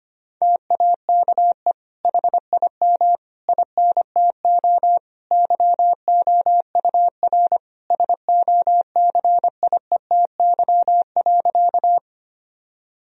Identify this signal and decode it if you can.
Morse code, heard as a keyed tone, about 25 wpm, 700 Hz. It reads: TAKE HIM INTO YOUR SOCIETY.
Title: Sentences from Top 1000 Words - Review Part 4 - 25wpm